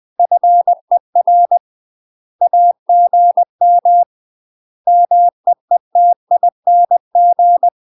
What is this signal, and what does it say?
Morse code, heard as a keyed tone, about 20 wpm, 700 Hz. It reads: FER AGM MEETING
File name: UK MASTER QSOs TRAINING / 20WPM / UK MASTER QSOs 0028